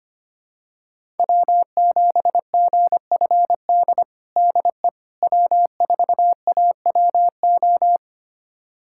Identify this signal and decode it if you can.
Morse code, heard as a keyed tone, about 25 wpm, 700 Hz. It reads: W7GFD DE W4AWO
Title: USA Callsign DE Callsign - 5 Characters - 25wpm